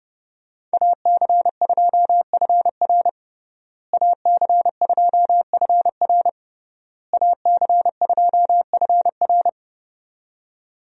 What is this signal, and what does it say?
Morse code, heard as a keyed tone, about 32 wpm, 700 Hz. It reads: AC2FR AC2FR AC2FR
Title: USA Call Signs - 5 Characters - Repeated 3x - 30wpm